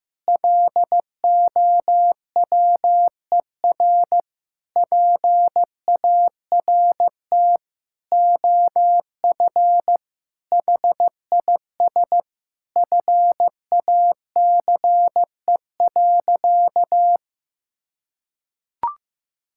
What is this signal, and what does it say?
Morse code, heard as a keyed tone, about 15 wpm, 700 Hz. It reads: LOWER PART OF HIS FACE.
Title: Sentences from Top 1000 Words - Review Part 5 - 15wpm